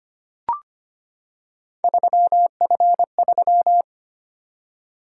Morse code, heard as a keyed tone, about 25 wpm, 700 Hz. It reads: 3F3
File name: Three Letter-Number Combinations - 25wpm